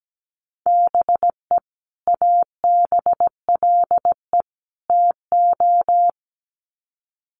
Morse code, heard as a keyed tone, about 17 wpm, 700 Hz. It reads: BE ABLE TO
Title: Sets of 3 Words 17wpm